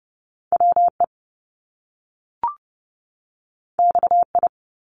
Morse code, heard as a keyed tone, about 30 wpm, 700 Hz. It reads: WE XI
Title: Top Two Letter Combinations in Words 30wpm